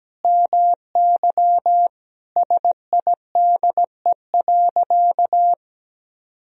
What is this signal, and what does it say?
Morse code, heard as a keyed tone, about 17 wpm, 700 Hz. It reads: MY SIDE.
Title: Sentences from Top 1000 Words - Review Part 1 - 17wpm